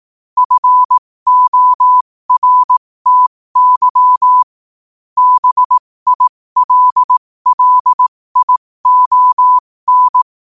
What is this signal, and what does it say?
Morse code, heard as a keyed tone, about 18 wpm, spaced as for 15 wpm, 1000 Hz. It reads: FORTY BILLION
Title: 15 WPM morse code quotes for Fri, 01 Mar 2024